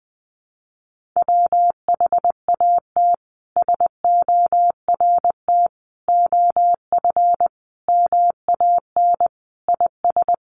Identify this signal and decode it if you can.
Morse code, heard as a keyed tone, about 20 wpm, 700 Hz. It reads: WHAT SORT OF MAN IS
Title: Sentences from Top 1300 Words 20wpm